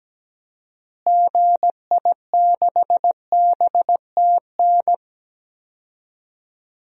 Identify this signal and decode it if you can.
Morse code, heard as a keyed tone, about 17 wpm, 700 Hz. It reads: GI6BTN
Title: DXCC-100 Call Signs - 6 Characters - Repeated 3x - 17wpm